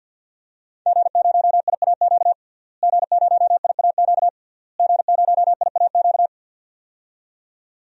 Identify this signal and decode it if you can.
Morse code, heard as a keyed tone, about 50 wpm, 700 Hz. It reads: G0IAQ G0IAQ G0IAQ
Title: DXCC-100 Call Signs - 5 Characters - Repeated 3x - 50wpm